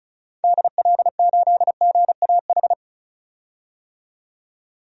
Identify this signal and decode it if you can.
Morse code, heard as a keyed tone, about 36 wpm, 700 Hz. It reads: DL8GAH
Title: DXCC-100 Call Signs - 6 Characters - 35wpm